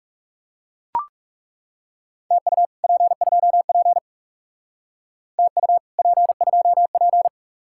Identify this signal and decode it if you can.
Morse code, heard as a keyed tone, about 35 wpm, 700 Hz. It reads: TU P2P TU P2P
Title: Speed-Racing Common QSO Phrases Encore - 30wpm